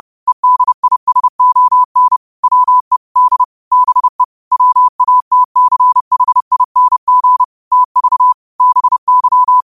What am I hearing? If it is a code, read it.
Morse code, heard as a keyed tone, about 30 wpm, 1000 Hz. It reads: EDISON WE D BE WATCHING TV BY
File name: qotdfiles / 30wpm